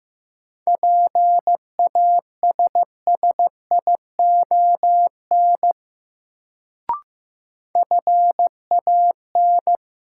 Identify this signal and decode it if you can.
Morse code, heard as a keyed tone, about 15 wpm, 700 Hz. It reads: PASSION FAN
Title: Sentences from Top 1300 Words - New Word Drill Alt - 15wpm